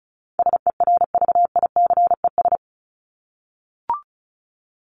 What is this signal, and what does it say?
Morse code, heard as a keyed tone, about 34 wpm, 700 Hz. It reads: SERVICES
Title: ES:S Suffix - Top 100 Words with Base Words 35wpm